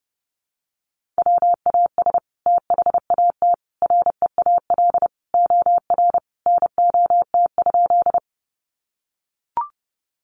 Morse code, heard as a keyed tone, about 30 wpm, 700 Hz. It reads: WAS THAT REAL OR NOT?
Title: Sentences from Top 600 Words 30wpm